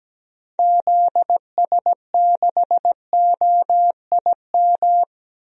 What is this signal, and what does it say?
Morse code, heard as a keyed tone, about 17 wpm, 700 Hz. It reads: ZS6OIM
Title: DXCC-100 Call Signs - 6 Characters - 17wpm